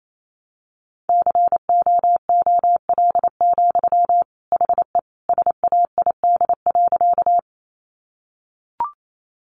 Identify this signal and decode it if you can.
Morse code, heard as a keyed tone, about 28 wpm, 700 Hz. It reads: COOL, HE SAID.
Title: Sentences from Top 1200 Words 28wpm